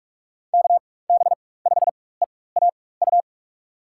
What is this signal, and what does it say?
Morse code, heard as a keyed tone, about 46 wpm, spaced as for 21 wpm, 700 Hz. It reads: KB5EAU
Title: USA Call Signs - 6 Characters - 4x Character Spacing - 45wpm